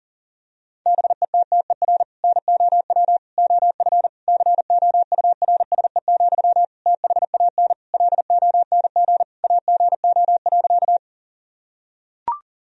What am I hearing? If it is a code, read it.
Morse code, heard as a keyed tone, about 37 wpm, 700 Hz. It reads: BETTER NOW OF COURSE, THAN LONG AGO.
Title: Sentences from Top 1000 Words - Review Part 3 - 40wpm